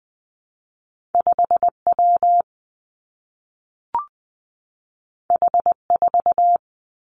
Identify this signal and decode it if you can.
Morse code, heard as a keyed tone, about 20 wpm, 700 Hz. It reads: 5 W H 4
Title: Two Challenging Letter-Number Combinations 20wpm